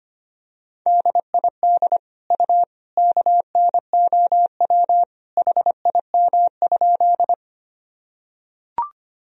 Morse code, heard as a keyed tone, about 25 wpm, 700 Hz. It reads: DID U KNOW HIM?
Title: Sentences from Top 100 Words and Abbreviations 25wpm